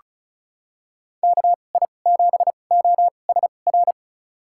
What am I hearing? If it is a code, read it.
Morse code, heard as a keyed tone, about 36 wpm, spaced as for 25 wpm, 700 Hz. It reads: KI7OSR
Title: USA Call Signs - 6 Characters - 2x Character Spacing - 35wpm